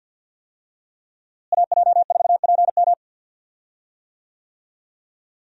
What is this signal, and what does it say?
Morse code, heard as a keyed tone, about 47 wpm, 700 Hz. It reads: AJ4PM
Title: USA Call Signs - 5 Characters - Repeated 3x - 50wpm